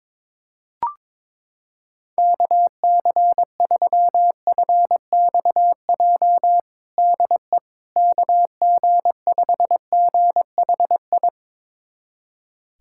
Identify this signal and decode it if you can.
Morse code, heard as a keyed tone, about 22 wpm, 700 Hz. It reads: KC3FXJ DE KG5GHI
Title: USA Callsign DE Callsign 22wpm